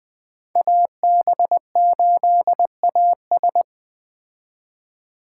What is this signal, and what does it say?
Morse code, heard as a keyed tone, about 20 wpm, 700 Hz. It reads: AB8AS
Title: 2023 POTA Activator Call Signs - 20wpm